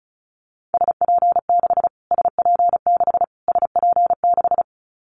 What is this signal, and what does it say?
Morse code, heard as a keyed tone, about 34 wpm, 700 Hz. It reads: SP6 SP6 SP6
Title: Callsign Prefixes for Top 25 DXCC - Repeated 3x - 35wpm